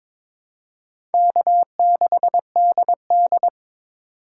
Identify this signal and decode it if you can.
Morse code, heard as a keyed tone, about 22 wpm, 700 Hz. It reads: K6DD
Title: USA Call Signs - 4 Characters - 4x Character Spacing - 22wpm